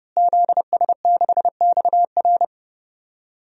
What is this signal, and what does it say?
Morse code, heard as a keyed tone, about 30 wpm, 700 Hz. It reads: ZS6XR
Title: DXCC-100 Call Signs - 5 Characters - Repeated 3x - 30wpm